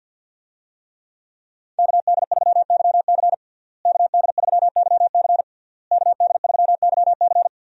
Morse code, heard as a keyed tone, about 52 wpm, 700 Hz. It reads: KD3YC KD3YC KD3YC
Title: USA Call Signs - 5 Characters - Repeated 3x - 50wpm